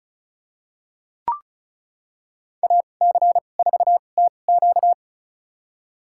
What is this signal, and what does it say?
Morse code, heard as a keyed tone, about 34 wpm, spaced as for 25 wpm, 700 Hz. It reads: AC4TQ
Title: USA Call Signs - 5 Characters - 2x Character Spacing - 35wpm